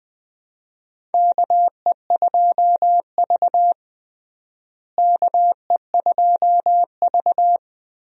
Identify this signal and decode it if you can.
Morse code, heard as a keyed tone, about 20 wpm, 700 Hz. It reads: KE2V KE2V
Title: USA Call Signs - 4 Characters - Repeated 2x - 20wpm